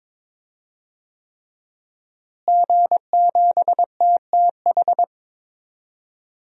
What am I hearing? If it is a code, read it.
Morse code, heard as a keyed tone, about 21 wpm, 700 Hz. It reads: G7TTH
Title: DXCC-100 Call Signs - 5 Characters - Repeated 3x - 22wpm